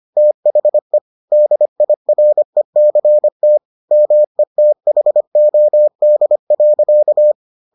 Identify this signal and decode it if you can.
Morse code, heard as a keyed tone, about 25 wpm, 600 Hz. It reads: THE DIRECT METHOD.